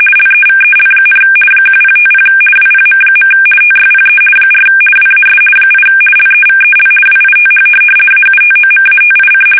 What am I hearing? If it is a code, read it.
RTTY, about 50 baud. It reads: LEVEL, 1050 WPM. TYPE 4 WAS 8 LEVEL WITH AUTOMATIC
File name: fsk850-50B / teletype-corp